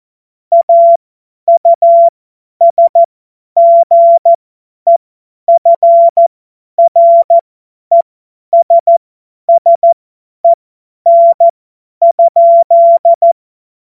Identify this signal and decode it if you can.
Morse code, heard as a keyed tone, about 14 wpm, spaced as for 10 wpm, 680 Hz. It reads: AUSGEFRESSEN?
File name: Morsekurs / Witze / Witz 10. Mai 2022 50 F70